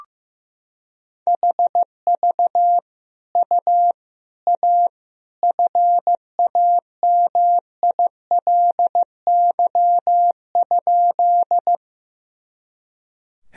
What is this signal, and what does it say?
Morse code, heard as a keyed tone, about 15 wpm, 700 Hz. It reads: HV U A FAMILY?
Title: Sentences from Top 400 Words and Abbreviations 15wpm